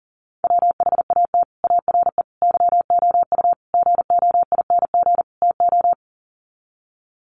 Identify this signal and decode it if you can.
Morse code, heard as a keyed tone, about 40 wpm, 700 Hz. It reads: WHAT ARE YOU GOING TO
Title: Sets of 5 Words 40wpm